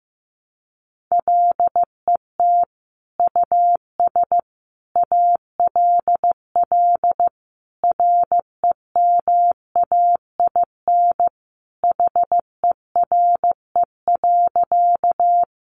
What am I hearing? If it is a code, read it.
Morse code, heard as a keyed tone, about 15 wpm, 700 Hz. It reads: LET US ALL REMAIN HERE.
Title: Sentences from Top 1200 Words 15wpm